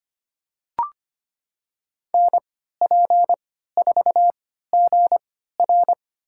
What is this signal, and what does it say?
Morse code, heard as a keyed tone, about 25 wpm, spaced as for 14 wpm, 700 Hz. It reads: NP4GR
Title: USA Call Signs - 3x Character Spacing - 25wpm